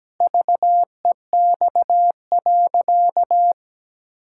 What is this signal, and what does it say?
Morse code, heard as a keyed tone, about 17 wpm, 700 Hz. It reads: VEX.
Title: Pangram Challenge v2 - 17wpm